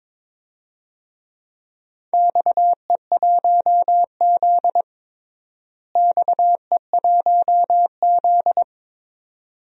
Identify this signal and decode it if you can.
Morse code, heard as a keyed tone, about 22 wpm, 700 Hz. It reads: XE1Z XE1Z
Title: DXCC-100 Call Signs - 4 Characters - Repeated 2x - 22wpm